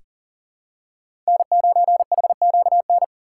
Morse code, heard as a keyed tone, about 40 wpm, 700 Hz. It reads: N9HQN
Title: USA Call Signs - 5 Characters - 4x Character Spacing - 40wpm